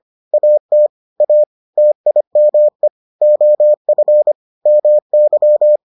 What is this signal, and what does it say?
Morse code, heard as a keyed tone, about 25 wpm, 600 Hz. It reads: AT A TIME OF MY